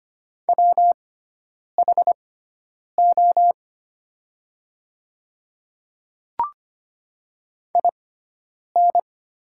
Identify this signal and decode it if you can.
Morse code, heard as a keyed tone, about 25 wpm, spaced as for 9 wpm, 700 Hz. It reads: WHO IN
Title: 6x Character Spacing - Top 100 Words - 25wpm